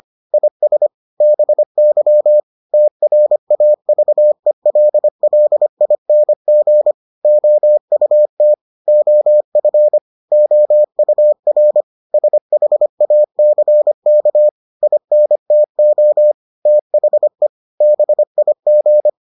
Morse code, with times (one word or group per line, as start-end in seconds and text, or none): 0.3-0.9 IS
1.2-2.4 BY
2.7-6.9 TRAVELLING
7.2-8.5 OUT
8.9-10.0 OF
10.3-11.8 OUR
12.1-14.5 SHACK
14.8-16.3 INTO
16.7-17.5 THE
17.8-19.1 BIG